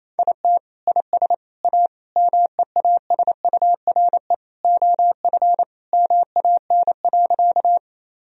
Morse code, heard as a keyed tone, about 28 wpm, 700 Hz. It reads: IT IS A MEASURE OF MAN.
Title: Sentences from Top 1000 Words - Review Part 2 - 28wpm